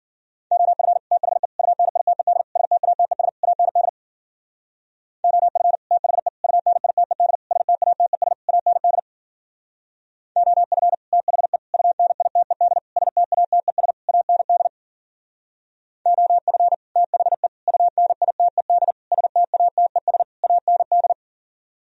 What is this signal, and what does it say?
Morse code, heard as a keyed tone, about 52 wpm, 700 Hz. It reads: OF THE UNITED STATES AND OF THE UNITED STATES AND OF THE UNITED STATES AND OF THE UNITED STATES AND